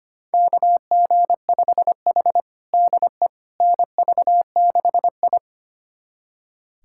Morse code, heard as a keyed tone, about 25 wpm, 700 Hz. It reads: KG5H DE NV6I
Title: USA Callsign DE Callsign - 4 Characters - 25wpm